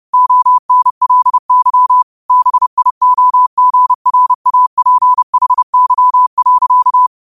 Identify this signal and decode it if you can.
Morse code, heard as a keyed tone, about 30 wpm, 1000 Hz. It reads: ONLY BIOGRAPHY.